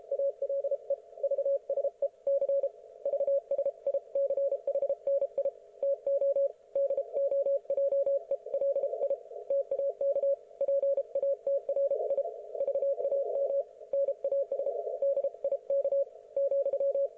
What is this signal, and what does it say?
Morse code, heard as a keyed tone, about 32 wpm, 560 Hz. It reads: ALE VSE C VSICHNI TO DOJELI TAK PATRI VAM NAS DIK ,
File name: Archiv / CUC117 / ok1fhi cul117